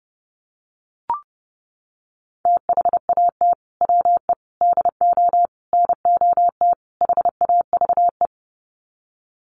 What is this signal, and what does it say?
Morse code, heard as a keyed tone, about 30 wpm, 700 Hz. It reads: THAT WE DO NOT HAVE
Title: Sets of 5 Words Encore 30wpm